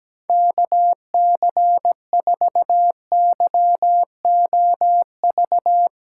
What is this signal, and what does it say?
Morse code, heard as a keyed tone, about 17 wpm, 700 Hz. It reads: KC4YOV
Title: USA Callsign DE Callsign - 6 Characters - 17wpm